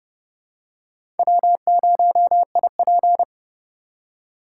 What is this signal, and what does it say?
Morse code, heard as a keyed tone, about 28 wpm, 700 Hz. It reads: W0IP